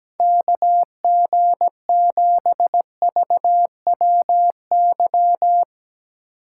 Morse code, heard as a keyed tone, about 17 wpm, 700 Hz. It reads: KG7VWY
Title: Speed Racing - USA 2x3 Call Signs 17wpm